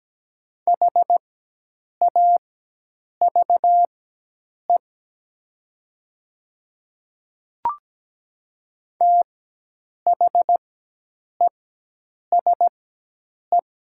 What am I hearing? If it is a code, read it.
Morse code, heard as a keyed tone, about 17 wpm, spaced as for 8 wpm, 700 Hz. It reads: HAVE THESE
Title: 4x Character Spacing - Top 100 Words - 17wpm